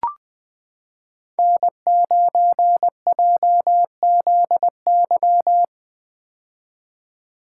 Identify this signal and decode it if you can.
Morse code, heard as a keyed tone, about 21 wpm, 700 Hz. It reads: N9JZY
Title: K1USN SST QSO Practice - 20wpm